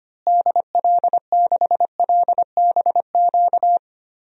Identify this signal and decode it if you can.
Morse code, heard as a keyed tone, about 25 wpm, 700 Hz. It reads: DL6LBQ
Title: DXCC-100 Call Signs - 6 Characters - Repeated 3x - 25wpm